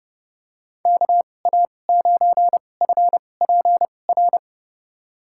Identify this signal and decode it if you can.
Morse code, heard as a keyed tone, about 31 wpm, spaced as for 22 wpm, 700 Hz. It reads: KA9FPR
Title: USA Call Signs - 6 Characters - 2x Character Spacing - 30wpm